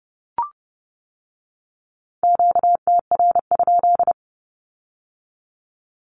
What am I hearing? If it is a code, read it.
Morse code, heard as a keyed tone, about 30 wpm, 700 Hz. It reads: QTR?
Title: Q-Codes with Spoken Meaning - 30wpm